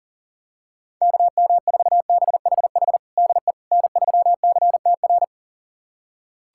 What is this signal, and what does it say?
Morse code, heard as a keyed tone, about 40 wpm, 700 Hz. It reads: KM4BHH DE N3CTR
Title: USA Callsign DE Callsign 40wpm